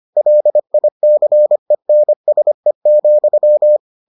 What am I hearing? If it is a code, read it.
Morse code, heard as a keyed tone, about 25 wpm, 600 Hz. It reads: LICENSE,